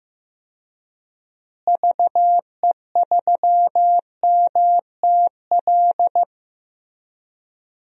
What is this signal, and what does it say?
Morse code, heard as a keyed tone, about 15 wpm, 700 Hz. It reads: VE3MTL
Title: DXCC-100 Call Signs - 6 Characters - Repeated 2x - 15wpm